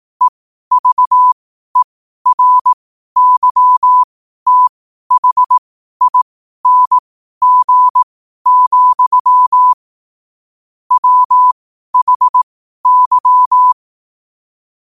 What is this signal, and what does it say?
Morse code, heard as a keyed tone, about 18 wpm, spaced as for 12 wpm, 1000 Hz. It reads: EVERYTHING, WHY
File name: qotdfiles / 12wpm